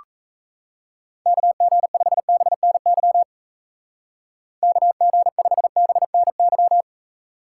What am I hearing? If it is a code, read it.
Morse code, heard as a keyed tone, about 35 wpm, 700 Hz. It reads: KG5BNY KG5BNY
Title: Speed Racing - USA 2x3 Call Signs 28wpm